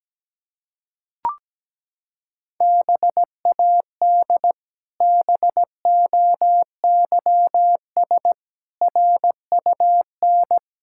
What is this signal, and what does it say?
Morse code, heard as a keyed tone, about 17 wpm, 700 Hz. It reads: BAD BOYS RUN
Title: Pangram Challenge v2 - 17wpm